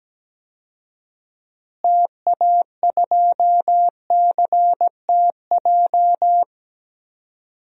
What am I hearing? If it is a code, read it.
Morse code, heard as a keyed tone, about 17 wpm, 700 Hz. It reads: TA2CTJ
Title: DXCC-100 Call Signs - 6 Characters - Repeated 2x - 17wpm